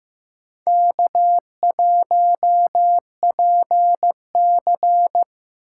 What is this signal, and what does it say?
Morse code, heard as a keyed tone, about 15 wpm, 700 Hz. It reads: K1PC
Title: USA Call Signs - 4 Characters - 8x Character Spacing - 15wpm